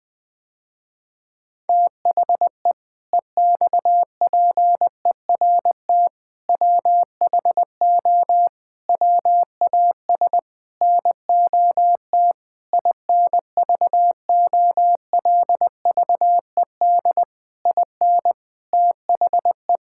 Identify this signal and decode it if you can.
Morse code, heard as a keyed tone, about 21 wpm, 700 Hz. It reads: THE EXPERT WHO WAS NOT INVOLVED IN THE
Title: Sentences with 7-Word Phrases - Spoken Beginning - 20wpm